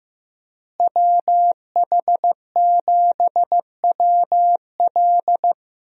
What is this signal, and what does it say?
Morse code, heard as a keyed tone, about 16 wpm, 700 Hz. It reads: WH7WL
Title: USA Call Signs - 5 Characters - Repeated 3x - 15wpm